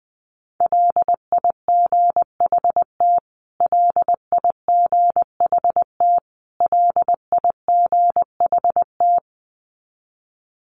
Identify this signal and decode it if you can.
Morse code, heard as a keyed tone, about 20 wpm, 700 Hz. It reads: LIGHT LIGHT LIGHT
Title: Top 200 Words Repeated 20wpm